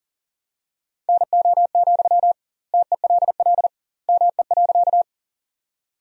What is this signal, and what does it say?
Morse code, heard as a keyed tone, about 40 wpm, 700 Hz. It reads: NO, TELL ME.
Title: 2x Word-Spacing - Sentences from Top 200 Words - 40wpm